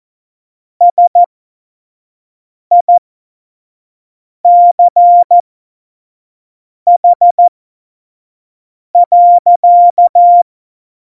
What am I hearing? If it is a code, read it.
Morse code, heard as a keyed tone, about 14 wpm, spaced as for 5 wpm, 700 Hz. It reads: SICH.
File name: Morsekurs / Witze / Witz 10.Juli 2018, 25 F70